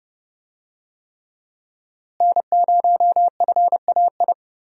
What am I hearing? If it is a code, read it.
Morse code, heard as a keyed tone, about 30 wpm, 700 Hz. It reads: N0FAI
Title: USA Call Signs - 5 Characters - Repeated 3x - 30wpm